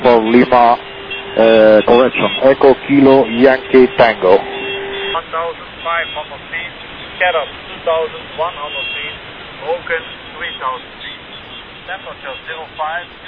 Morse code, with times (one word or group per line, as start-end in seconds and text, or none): none